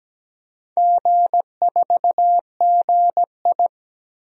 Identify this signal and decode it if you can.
Morse code, heard as a keyed tone, about 17 wpm, 700 Hz. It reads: G4GI
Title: DXCC-100 Call Signs - 4 Characters - Repeated 3x - 17wpm